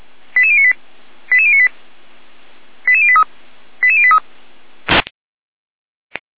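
Five-tone selective call (ZVEI), heard as quick runs of five tones, tones of about 70 ms. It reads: 88098 88098 88083 88083
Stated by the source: ZVEI five-tone call 88571 received on SDR.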